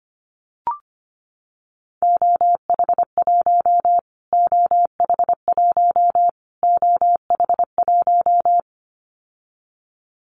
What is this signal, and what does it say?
Morse code, heard as a keyed tone, about 25 wpm, 700 Hz. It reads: OH1 OH1 OH1
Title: Callsign Prefixes for Top 25 DXCC - Repeated 3x - 25wpm